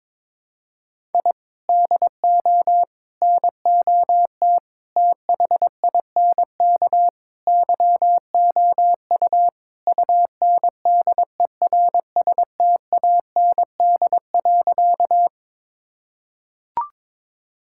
Morse code, heard as a keyed tone, about 22 wpm, 700 Hz. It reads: I DO NOT THINK YOU UNDERSTAND.
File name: practice / 22wpm / Sentences with 4-Word Phrases - 22wpm